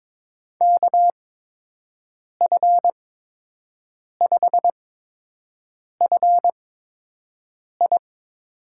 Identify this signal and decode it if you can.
Morse code, heard as a keyed tone, about 23 wpm, spaced as for 6 wpm, 700 Hz. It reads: KF5FI
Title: USA Call Signs - 6 Characters - 8x Character Spacing - 22wpm